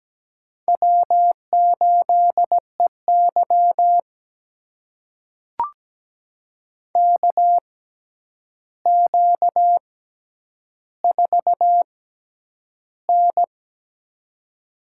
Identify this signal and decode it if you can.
Morse code, heard as a keyed tone, about 17 wpm, spaced as for 9 wpm, 700 Hz. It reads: W8EY KQ4N
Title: USA Call Signs - 4 Characters - 6x Character Spacing - 17wpm